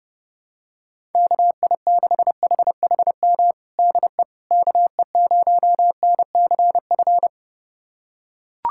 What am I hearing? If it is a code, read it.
Morse code, heard as a keyed tone, about 30 wpm, 700 Hz. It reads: KI6HHM DE KE0NCF
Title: USA Callsign DE Callsign 30wpm